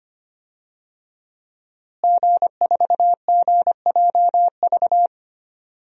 Morse code, heard as a keyed tone, about 25 wpm, 700 Hz. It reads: G4GJV
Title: DXCC-100 Call Signs - 5 Characters - Repeated 3x - 25wpm